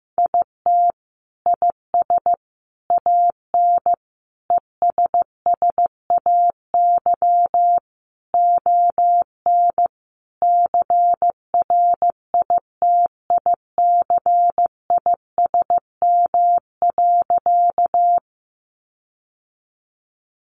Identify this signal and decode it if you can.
Morse code, heard as a keyed tone, about 15 wpm, 700 Hz. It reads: IT IS AN ESSAY ON CRITICISM.
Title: Sentences from Top 1500 Words 15wpm